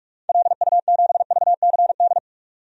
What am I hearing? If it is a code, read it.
Morse code, heard as a keyed tone, about 47 wpm, 700 Hz. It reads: LU7VCD
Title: DXCC-100 Call Signs - 6 Characters - 45wpm